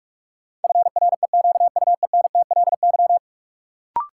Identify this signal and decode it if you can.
Morse code, heard as a keyed tone, about 41 wpm, 700 Hz. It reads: FREQUENTLY
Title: Sentences from Top 1300 Words - New Word Drill Alt - 45wpm